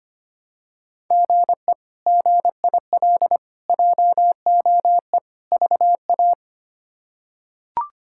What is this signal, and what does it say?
Morse code, heard as a keyed tone, about 25 wpm, 700 Hz. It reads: GE GIL JOE VA
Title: K1USN SST QSO Practice - 25wpm